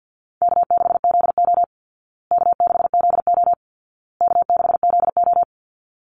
Morse code, heard as a keyed tone, about 48 wpm, 700 Hz. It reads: K6ZO K6ZO K6ZO
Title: Easy Call Signs Repeated 3x - 50wpm